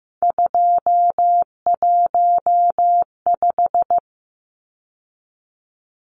Morse code, heard as a keyed tone, about 15 wpm, 700 Hz. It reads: 215
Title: Numbers - 3-Digit 15wpm